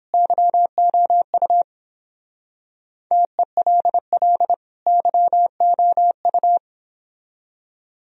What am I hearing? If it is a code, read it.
Morse code, heard as a keyed tone, about 26 wpm, 700 Hz. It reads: YOU TELL YOU
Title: Speed-Racing - Sets of 2 Words 22wpm